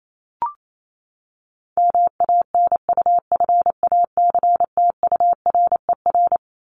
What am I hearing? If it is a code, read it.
Morse code, heard as a keyed tone, about 28 wpm, 700 Hz. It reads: MANUFACTURER
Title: Top 1751-2000 Words 28wpm